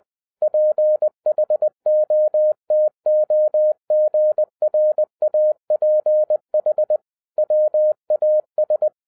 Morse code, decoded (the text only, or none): PHOTOGRAPH WAS